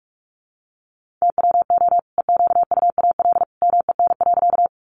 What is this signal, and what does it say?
Morse code, heard as a keyed tone, about 44 wpm, 700 Hz. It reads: TWO EQUAL MEN.
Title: Sentences from Top 1400 Words 45wpm